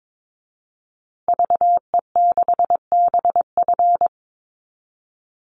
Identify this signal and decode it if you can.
Morse code, heard as a keyed tone, about 22 wpm, 700 Hz. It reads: VE6BF
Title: Call Signs Repeated 2x - 22wpm